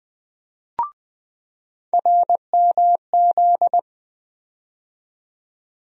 Morse code, heard as a keyed tone, about 20 wpm, 700 Hz. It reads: RMZ
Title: USA Call Sign Postfixes - 3 Letters 20wpm